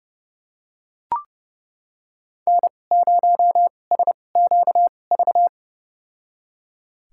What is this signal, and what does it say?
Morse code, heard as a keyed tone, about 30 wpm, spaced as for 22 wpm, 700 Hz. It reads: N0SQV
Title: USA Call Signs - 2x Character Spacing - 30wpm